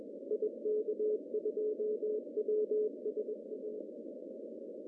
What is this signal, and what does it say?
Morse code, heard as a keyed tone, about 22 wpm, 410 Hz. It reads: IK2WSA
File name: CW / cqww-cw-k5tr